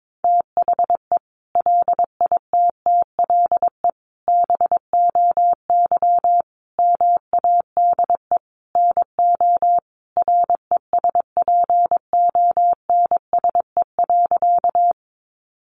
Morse code, heard as a keyed tone, about 22 wpm, 700 Hz. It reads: THE LITTLE BOY MADE NO RESPONSE.
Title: Sentences from Top 1400 Words 22wpm